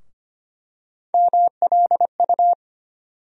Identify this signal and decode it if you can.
Morse code, heard as a keyed tone, about 25 wpm, 700 Hz. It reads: MLU
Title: USA Call Sign Postfixes - 3 Letters 25wpm